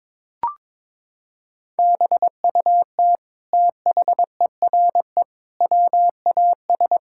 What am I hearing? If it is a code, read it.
Morse code, heard as a keyed tone, about 22 wpm, 700 Hz. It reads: BUT THERE WAS
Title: Sentences from Top 1600 Words 22wpm